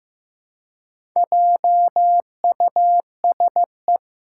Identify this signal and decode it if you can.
Morse code, heard as a keyed tone, about 15 wpm, 700 Hz. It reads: JUSE
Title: Four Challenging Letter-Number Combinations 15wpm